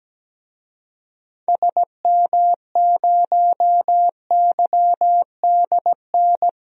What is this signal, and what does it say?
Morse code, heard as a keyed tone, about 17 wpm, 700 Hz. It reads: SM0YDN
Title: DXCC-100 Call Signs - 6 Characters - Repeated 2x - 17wpm